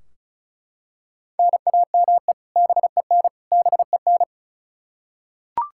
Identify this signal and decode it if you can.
Morse code, heard as a keyed tone, about 36 wpm, 700 Hz. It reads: NAME BEN BEN
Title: Practice QSO - Exchange 1 - Beginning - 35wpm